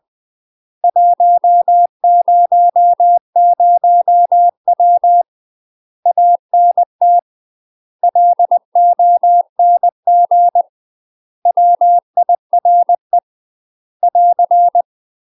Morse code, heard as a keyed tone, about 20 wpm, 700 Hz. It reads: 100W ANT LONG WIRE <AR>
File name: UK MASTER QSOs TRAINING / 20WPM / UK MASTER QSOs 0061